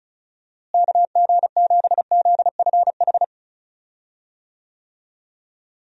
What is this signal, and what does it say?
Morse code, heard as a keyed tone, about 36 wpm, 700 Hz. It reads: KG7ZFH
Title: USA Call Signs - 6 Characters - Repeated 2x - 35wpm